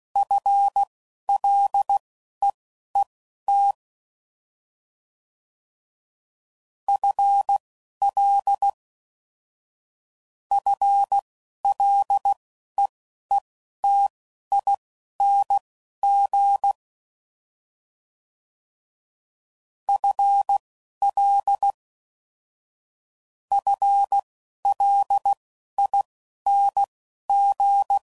Morse code, heard as a keyed tone, about 16 wpm, spaced as for 11 wpm, 800 Hz. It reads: FLEET FL FLEETING FL FLING
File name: Prefixes / FLs & THs / FL Prefixes 1